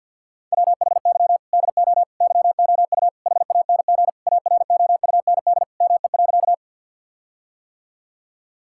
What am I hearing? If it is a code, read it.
Morse code, heard as a keyed tone, about 50 wpm, 700 Hz. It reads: WHY DO YOU HANG AROUND ME.
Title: Sentences from Top 1600 Words 50wpm